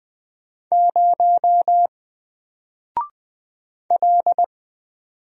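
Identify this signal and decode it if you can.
Morse code, heard as a keyed tone, about 20 wpm, 700 Hz. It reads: 0 L
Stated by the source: Single letter and number Morse drill at 20 wpm, from N4LCD.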